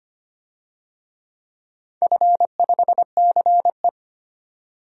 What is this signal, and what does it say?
Morse code, heard as a keyed tone, about 25 wpm, 700 Hz. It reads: F5CE
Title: DXCC-100 Call Signs - 4 Characters - 25wpm